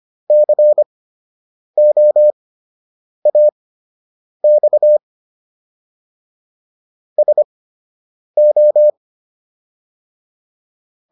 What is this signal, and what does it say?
Morse code, heard as a keyed tone, about 25 wpm, spaced as for 8 wpm, 600 Hz. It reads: COAX SO